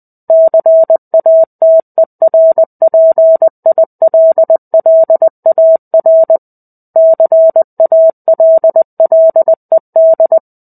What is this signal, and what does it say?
Morse code, heard as a keyed tone, about 20 wpm, 650 Hz. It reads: CATERPILLAR CALLED
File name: 05-Chapter-05 20 WPM